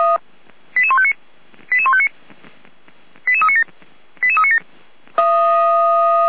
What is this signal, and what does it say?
Five-tone selective call (ZVEI), heard as quick runs of five tones, tones of about 70 ms. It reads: 88159 88159 88397 88397
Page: /455/SDR/ZVEI/88437/2024.06.30/